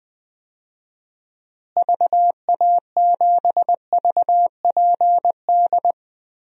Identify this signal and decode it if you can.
Morse code, heard as a keyed tone, about 20 wpm, 700 Hz. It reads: VA7VPD
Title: DXCC-100 Call Signs - 6 Characters - Repeated 3x - 20wpm